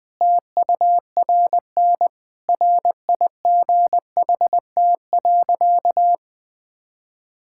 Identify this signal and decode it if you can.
Morse code, heard as a keyed tone, about 19 wpm, 700 Hz. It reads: TURN RIGHT.
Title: Sentences from Top 1000 Words - Review Part 5 - 20wpm